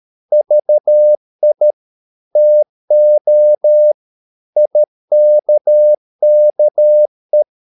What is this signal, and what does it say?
Morse code, heard as a keyed tone, about 13 wpm, 600 Hz. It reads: VI TO IKKE
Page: Ferie 13 wpm